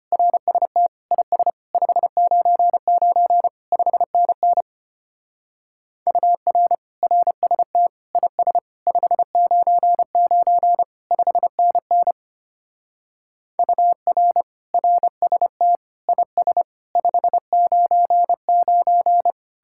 Morse code, heard as a keyed tone, about 30 wpm, 700 Hz. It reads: RST IS 599 5NN UR RST IS 599 5NN UR RST IS 599
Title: Speed-Racing Common QSO Phrases - 25wpm